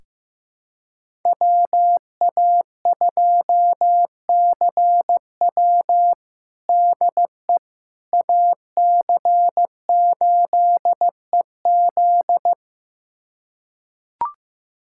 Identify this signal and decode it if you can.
Morse code, heard as a keyed tone, about 15 wpm, 700 Hz. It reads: WA2CW DE AC8EZ
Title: USA Callsign DE Callsign - 5 Characters - 15wpm